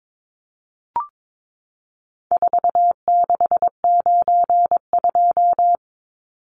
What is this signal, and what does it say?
Morse code, heard as a keyed tone, about 22 wpm, 700 Hz. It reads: 4692
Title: Numbers - 4-Digit 22wpm